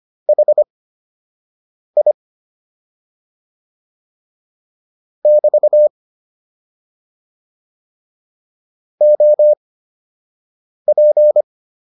Morse code, heard as a keyed tone, about 25 wpm, spaced as for 6 wpm, 600 Hz. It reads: HI <BT> OP